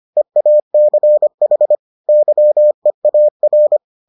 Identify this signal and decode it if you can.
Morse code, heard as a keyed tone, about 25 wpm, 600 Hz. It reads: EACH YEAR